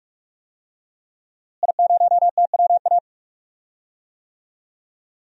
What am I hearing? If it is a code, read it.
Morse code, heard as a keyed tone, about 46 wpm, 700 Hz. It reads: I0TWA
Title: DXCC-100 Call Signs - 5 Characters - Repeated 3x - 45wpm